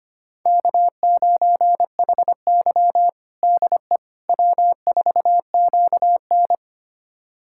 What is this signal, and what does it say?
Morse code, heard as a keyed tone, about 25 wpm, 700 Hz. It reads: K9HY DE W4QN
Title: USA Callsign DE Callsign - 4 Characters - 25wpm